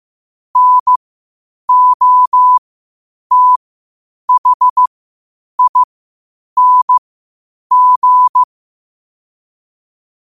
Morse code, heard as a keyed tone, about 15 wpm, spaced as for 8 wpm, 1000 Hz. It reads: NOTHING
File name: qotdfiles / 8wpm